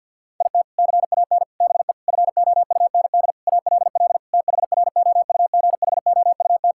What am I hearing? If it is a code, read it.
Morse code, heard as a keyed tone, about 47 wpm, 690 Hz. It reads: IT CAN BE FOUND ALL THROUGHOUT
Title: Sentences from Top 1000 Words - Review Part 5 - 50wpm